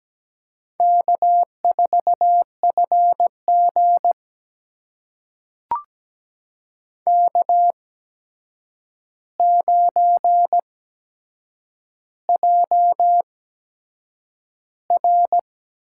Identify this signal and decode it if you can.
Morse code, heard as a keyed tone, about 17 wpm, spaced as for 7 wpm, 700 Hz. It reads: K4FG K9JR
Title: USA Call Signs - 4 Characters - 8x Character Spacing - 17wpm